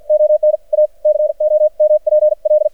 Morse code, heard as a keyed tone, about 53 wpm, spaced as for 44 wpm, 610 Hz. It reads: ON A COMPL
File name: archived / CRSnet